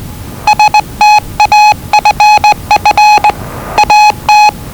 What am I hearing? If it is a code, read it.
Morse code, heard as a keyed tone, about 20 wpm, 840 Hz. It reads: STAFF AT
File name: WellingtonRadioZlwCloseDownOnHfAndMf / 01-Zlw 4277khz 417.5 qrt until 0925z